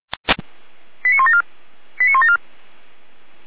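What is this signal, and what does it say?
Five-tone selective call (ZVEI), heard as quick runs of five tones, tones of about 70 ms. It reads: 89174 89174
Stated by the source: ZVEI five-tone selective call received on SDR.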